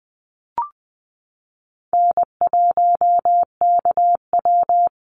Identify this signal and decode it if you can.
Morse code, heard as a keyed tone, about 20 wpm, 700 Hz. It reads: N1KW
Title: Easy Call Signs - 20wpm